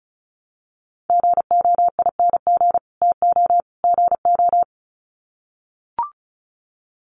Morse code, heard as a keyed tone, about 34 wpm, 700 Hz. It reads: GOING TO GO
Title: Sets of 3 Words 35wpm